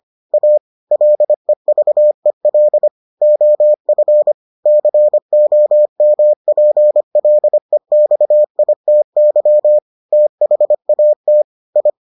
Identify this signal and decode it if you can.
Morse code, heard as a keyed tone, about 25 wpm, 600 Hz. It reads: A LEVEL OF COMPLEXITY THAT I